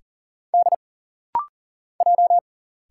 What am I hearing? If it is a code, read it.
Morse code, heard as a keyed tone, about 39 wpm, 700 Hz. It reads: D J
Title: Single Letters - Rapid-Fire 40wpm